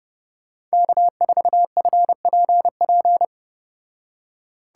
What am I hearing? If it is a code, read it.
Morse code, heard as a keyed tone, about 32 wpm, 700 Hz. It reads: K4FPP